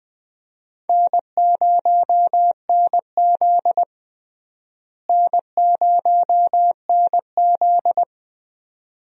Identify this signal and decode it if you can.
Morse code, heard as a keyed tone, about 20 wpm, 700 Hz. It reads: N0NZ N0NZ
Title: USA Call Signs - 4 Characters - Repeated 2x - 20wpm